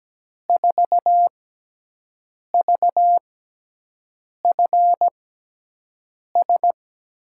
Morse code, heard as a keyed tone, about 17 wpm, spaced as for 6 wpm, 700 Hz. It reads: 4VFS
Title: USA Call Signs - 6x Character Spacing - 17wpm